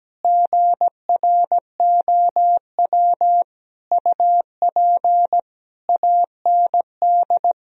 Morse code, heard as a keyed tone, about 17 wpm, 700 Hz. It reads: GROW UP AND
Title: Sentences from Top 1000 Words - Review Part 1 - 17wpm